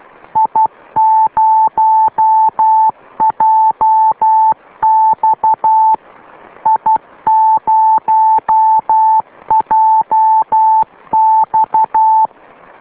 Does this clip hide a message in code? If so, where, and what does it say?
Yes: Morse code from 0.4 to 12.3 s, heard as a keyed tone, about 12 wpm, 880 Hz. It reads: I0JX I0JX